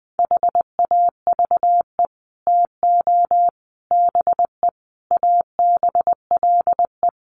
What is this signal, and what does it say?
Morse code, heard as a keyed tone, about 20 wpm, 700 Hz. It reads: HAVE TO BE ABLE
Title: Sets of 5 Words 20wpm